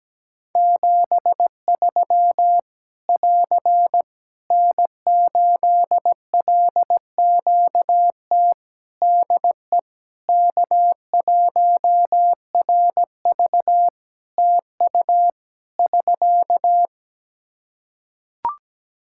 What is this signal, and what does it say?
Morse code, heard as a keyed tone, about 17 wpm, 700 Hz. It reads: 73 <AR> N8LQT DE K1RV TU <SK>
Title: K1USN SST QSO Practice 17wpm